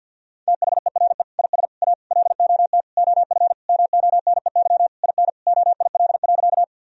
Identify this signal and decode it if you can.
Morse code, heard as a keyed tone, about 50 wpm, 700 Hz. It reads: THERE IS A LOT OF MONEY IN OIL.
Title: Sentences from Top 1000 Words - Review Part 4 - 50wpm